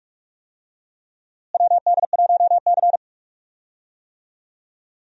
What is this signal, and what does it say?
Morse code, heard as a keyed tone, about 46 wpm, 700 Hz. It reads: WD1C
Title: USA Call Signs - 4 Characters - Repeated 2x - 45wpm